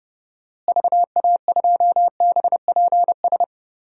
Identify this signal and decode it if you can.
Morse code, heard as a keyed tone, about 30 wpm, 700 Hz. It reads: VA2BPS
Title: DXCC-100 Call Signs - 6 Characters - Repeated 2x - 30wpm